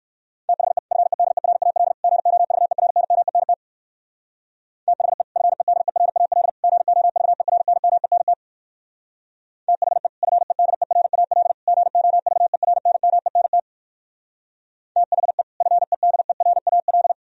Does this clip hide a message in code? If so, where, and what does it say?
Yes: Morse code from 0.5 to 17.1 s, heard as a keyed tone, about 58 wpm, 700 Hz. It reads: THE FEDERAL GOVERNMENT THE FEDERAL GOVERNMENT THE FEDERAL GOVERNMENT THE FEDERAL